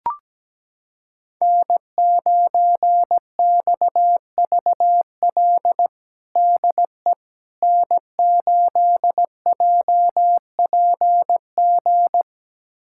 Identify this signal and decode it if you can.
Morse code, heard as a keyed tone, about 17 wpm, 700 Hz. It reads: N9XVL DE N8JPG
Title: USA Callsign DE Callsign - 5 Characters - 17wpm